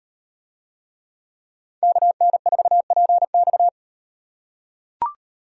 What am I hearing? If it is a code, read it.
Morse code, heard as a keyed tone, about 38 wpm, 700 Hz. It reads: KN4PX